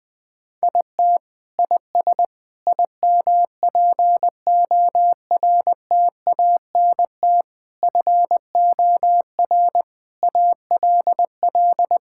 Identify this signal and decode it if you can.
Morse code, heard as a keyed tone, about 20 wpm, 700 Hz. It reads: IT IS IMPORTANT FOR ALL
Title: Sentences with 4-Word Phrases - 20wpm